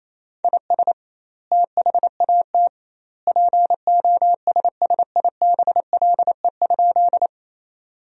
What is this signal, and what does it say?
Morse code, heard as a keyed tone, about 28 wpm, 700 Hz. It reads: IS THAT POSSIBLE?
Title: 2x Word-Spacing - Sentences from Top 500 Words - 28wpm